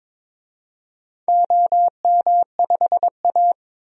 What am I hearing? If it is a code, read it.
Morse code, heard as a keyed tone, about 23 wpm, 700 Hz. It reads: OM5A
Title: DXCC-100 Call Signs - 4 Characters - 22wpm